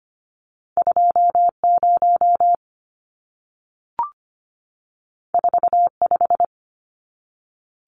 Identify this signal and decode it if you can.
Morse code, heard as a keyed tone, about 25 wpm, 700 Hz. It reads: 20 45
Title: Numbers - 2-Digit 25wpm